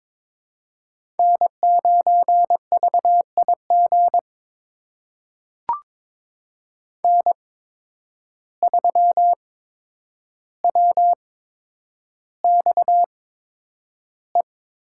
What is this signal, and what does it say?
Morse code, heard as a keyed tone, about 22 wpm, spaced as for 9 wpm, 700 Hz. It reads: N9VIG N3WXE
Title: USA Call Signs - 5 Characters - 8x Character Spacing - 22wpm